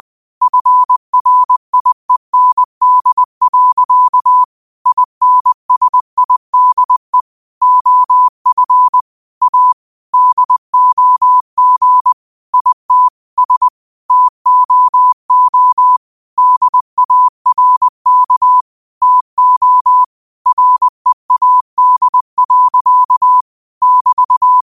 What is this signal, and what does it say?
Morse code, heard as a keyed tone, about 20 wpm, 1000 Hz. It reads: FRIEND. INSIDE OF A DOG IT S TOO DARK TO READ. <BT>
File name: qotdfiles / 20wpm